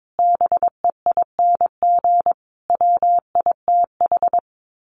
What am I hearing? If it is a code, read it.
Morse code, heard as a keyed tone, about 22 wpm, 700 Hz. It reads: BEING WITH
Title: Sets of 5 Words 22wpm